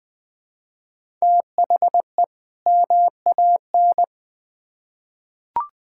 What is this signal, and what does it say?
Morse code, heard as a keyed tone, about 21 wpm, 700 Hz. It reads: THE MAN
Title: Build A Phrase - 2-Word Phrases - 20wpm